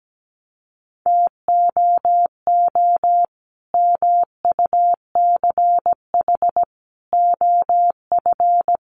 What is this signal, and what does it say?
Morse code, heard as a keyed tone, about 17 wpm, 700 Hz. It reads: TOO MUCH OF
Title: Sentences from Top 200 Words 17wpm